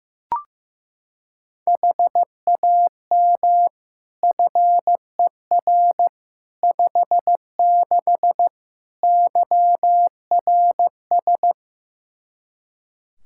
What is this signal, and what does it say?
Morse code, heard as a keyed tone, about 15 wpm, 700 Hz. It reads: HAM FER 56 YRS
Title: Practice QSO - Exchange 3 - More Info - 15wpm